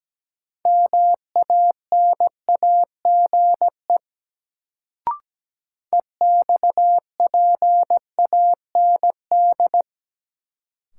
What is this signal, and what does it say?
Morse code, heard as a keyed tone, about 17 wpm, 700 Hz. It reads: MANAGE EXPAND
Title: Sentences from Top 1400 Words - New Word Drill - 17wpm